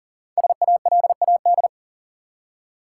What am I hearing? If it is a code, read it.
Morse code, heard as a keyed tone, about 38 wpm, 700 Hz. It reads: SALAD
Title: Five Letter Words - 40wpm